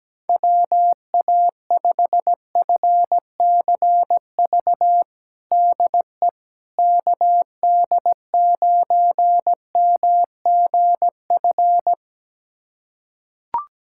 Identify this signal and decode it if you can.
Morse code, heard as a keyed tone, about 17 wpm, 700 Hz. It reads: WA5FCV DE KD9MGF
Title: USA Callsign DE Callsign - 6 Characters - 17wpm